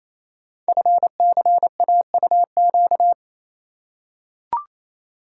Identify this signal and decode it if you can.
Morse code, heard as a keyed tone, about 29 wpm, 700 Hz. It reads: FCAUQ
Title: Code Groups - 5 Letters - 28wpm